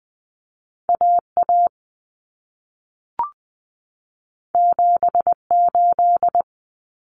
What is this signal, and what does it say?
Morse code, heard as a keyed tone, about 20 wpm, 700 Hz. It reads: AA 78
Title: Common Cut Numbers - 2-Digit 20wpm